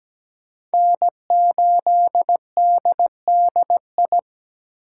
Morse code, heard as a keyed tone, about 20 wpm, 700 Hz. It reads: N8DDI
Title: Speed Racing - USA Call Signs 17wpm